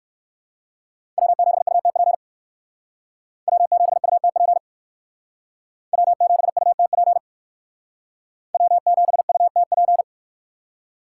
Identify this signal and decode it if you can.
Morse code, heard as a keyed tone, about 57 wpm, 700 Hz. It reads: W7UTP W7UTP W7UTP W7UTP